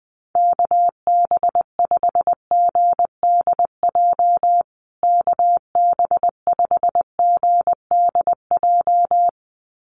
Morse code, heard as a keyed tone, about 20 wpm, 700 Hz. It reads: KB5GDJ KB5GDJ
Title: USA Call Signs Repeated 2x 20wpm